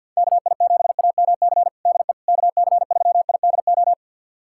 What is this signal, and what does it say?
Morse code, heard as a keyed tone, about 51 wpm, 700 Hz. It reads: KI7AMC DE KC3IDO
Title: USA Callsign DE Callsign 50wpm